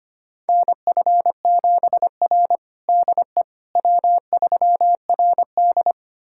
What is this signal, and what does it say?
Morse code, heard as a keyed tone, about 25 wpm, 700 Hz. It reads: NF7R DE W3RD
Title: USA Callsign DE Callsign - 4 Characters - 25wpm